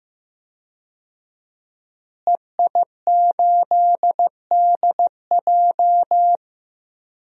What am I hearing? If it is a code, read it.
Morse code, heard as a keyed tone, about 15 wpm, 700 Hz. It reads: EI8DJ
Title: DXCC-100 Call Signs - 5 Characters - Repeated 2x - 15wpm